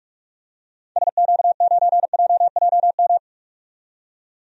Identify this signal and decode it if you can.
Morse code, heard as a keyed tone, about 44 wpm, 700 Hz. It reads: SQ9JJM
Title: DXCC-100 Call Signs - 6 Characters - 45wpm